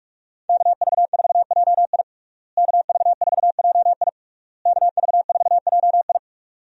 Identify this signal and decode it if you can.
Morse code, heard as a keyed tone, about 43 wpm, 700 Hz. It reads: KV4JI KV4JI KV4JI
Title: USA Call Signs - 5 Characters - Repeated 3x - 45wpm